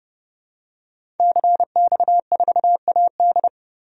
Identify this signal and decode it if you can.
Morse code, heard as a keyed tone, about 30 wpm, 700 Hz. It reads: CX4AD
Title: DXCC-100 Call Signs - 5 Characters - 30wpm